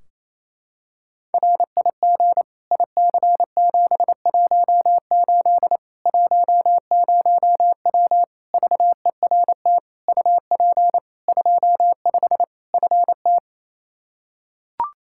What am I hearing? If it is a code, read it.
Morse code, heard as a keyed tone, about 28 wpm, 700 Hz. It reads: RIG IC718 10W VERT UP 25 FT
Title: Practice QSO - Exchange 2 - Info - 28wpm